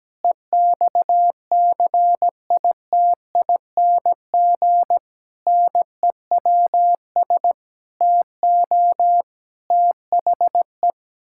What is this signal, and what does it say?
Morse code, heard as a keyed tone, about 17 wpm, 700 Hz. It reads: EXCITING NEWS TO THE
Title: Pangram Challenge v2 - 17wpm